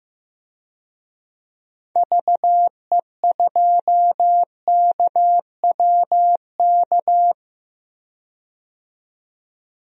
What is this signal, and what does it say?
Morse code, heard as a keyed tone, about 15 wpm, 700 Hz. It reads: VE2KWK
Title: DXCC-100 Call Signs - 6 Characters - 15wpm